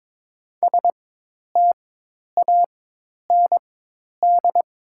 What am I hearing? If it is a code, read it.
Morse code, heard as a keyed tone, about 22 wpm, spaced as for 10 wpm, 700 Hz. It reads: STAND
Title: 4x Character Spacing - Top 500 Words - 22wpm